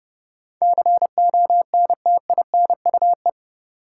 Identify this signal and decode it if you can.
Morse code, heard as a keyed tone, about 30 wpm, 700 Hz. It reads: CONTINUE
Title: Sentences from Top 1100 Words - New Word Drill Alt - 30wpm